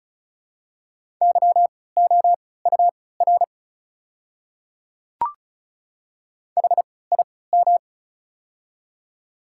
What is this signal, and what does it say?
Morse code, heard as a keyed tone, about 36 wpm, spaced as for 20 wpm, 690 Hz. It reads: YOUR HIM
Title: 3x Character Spacing - Top 100 Words - 35wpm